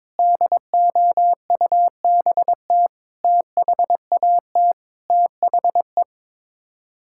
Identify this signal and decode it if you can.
Morse code, heard as a keyed tone, about 22 wpm, 700 Hz. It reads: DOUBT THAT THE
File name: practice / 22wpm / Sets of 6 Words - 22wpm